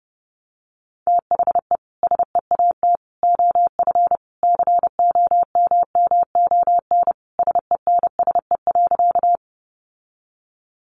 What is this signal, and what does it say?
Morse code, heard as a keyed tone, about 30 wpm, 700 Hz. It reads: THE SEAT OF COMMON SENSE.
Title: Sentences from Top 1500 Words 30wpm